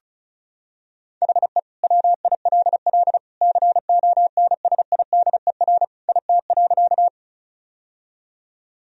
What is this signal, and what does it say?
Morse code, heard as a keyed tone, about 34 wpm, 700 Hz. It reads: HE WILL CONSIDER IT.
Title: Sentences from Top 1000 Words - Review Part 4 - 35wpm